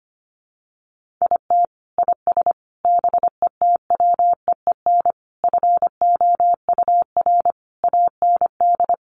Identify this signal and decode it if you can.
Morse code, heard as a keyed tone, about 25 wpm, 700 Hz. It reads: IT IS BETWEEN FOUR AND
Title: Sentences from Top 1500 Words 25wpm